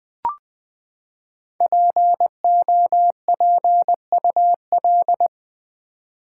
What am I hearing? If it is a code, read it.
Morse code, heard as a keyed tone, about 20 wpm, 700 Hz. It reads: POPUL
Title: Build A Word - 5 Letters At A Time - 10-Letter Words - 20wpm